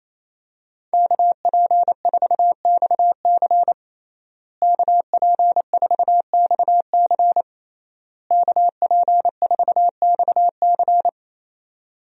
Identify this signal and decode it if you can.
Morse code, heard as a keyed tone, about 27 wpm, 700 Hz. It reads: KP4XC KP4XC KP4XC
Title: DXCC-100 Call Signs - 5 Characters - Repeated 3x - 28wpm